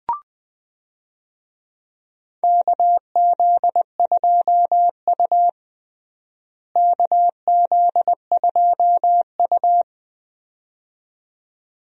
Morse code, heard as a keyed tone, about 20 wpm, 700 Hz. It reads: KZ2U KZ2U
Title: USA Call Signs - 4 Characters - Repeated 2x - 20wpm